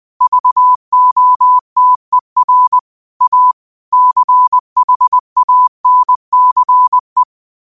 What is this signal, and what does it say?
Morse code, heard as a keyed tone, about 20 wpm, 1000 Hz. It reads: VOTER A CHANCE
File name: qotdfiles / 20wpm